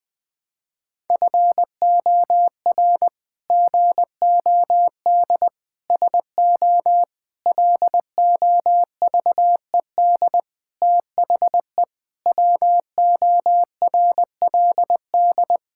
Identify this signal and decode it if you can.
Morse code, heard as a keyed tone, about 20 wpm, 700 Hz. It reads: FOR GOD SO LOVED THE WORLD
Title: Sets of 8 Words - 20wpm